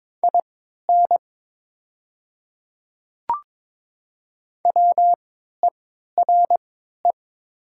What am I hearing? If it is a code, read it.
Morse code, heard as a keyed tone, about 23 wpm, spaced as for 13 wpm, 700 Hz. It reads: IN WERE
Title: 3x Character Spacing - Top 300 Words - 22wpm